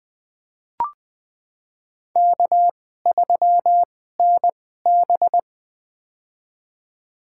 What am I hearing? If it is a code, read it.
Morse code, heard as a keyed tone, about 20 wpm, spaced as for 14 wpm, 700 Hz. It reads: K3NB
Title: USA Call Signs - 4 Characters - 2x Character Spacing - 20wpm